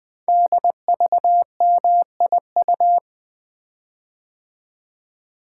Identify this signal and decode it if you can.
Morse code, heard as a keyed tone, about 20 wpm, 700 Hz. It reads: DVMIU
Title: Code Groups - 5 Letters - 20wpm